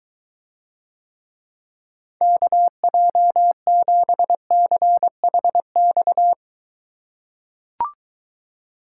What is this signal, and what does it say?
Morse code, heard as a keyed tone, about 23 wpm, 700 Hz. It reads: KJ7CHX